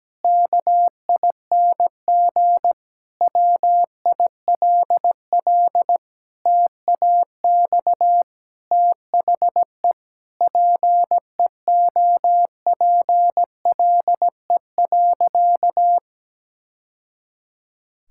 Morse code, heard as a keyed tone, about 17 wpm, 700 Hz. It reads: KING WILL TAX THE PEOPLE.
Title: Sentences from Top 1000 Words - Review Part 6 - 17wpm